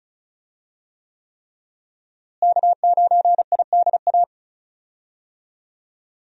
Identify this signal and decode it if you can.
Morse code, heard as a keyed tone, about 36 wpm, 700 Hz. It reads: K9IDA
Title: USA Call Signs - 5 Characters - Repeated 3x - 35wpm